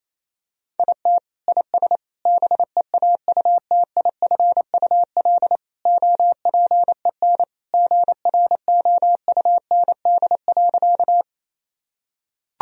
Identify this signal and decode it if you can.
Morse code, heard as a keyed tone, about 27 wpm, 700 Hz. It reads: IT IS BEAUTIFUL OPEN GROUND.
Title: Sentences from Top 1000 Words - Review Part 2 - 28wpm